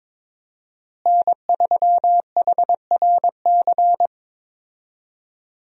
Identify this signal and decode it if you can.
Morse code, heard as a keyed tone, about 22 wpm, 700 Hz. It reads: N3HRC
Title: USA Call Signs - 5 Characters - Repeated 3x - 22wpm